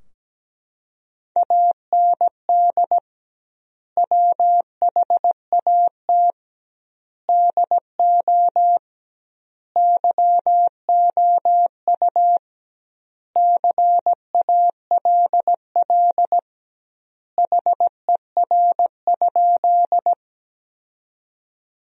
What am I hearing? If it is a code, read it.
Morse code, heard as a keyed tone, about 17 wpm, 700 Hz. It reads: AND WHAT DO YOU CALL HER?
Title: 2x Word-Spacing - Sentences from Top 100 Words - 17wpm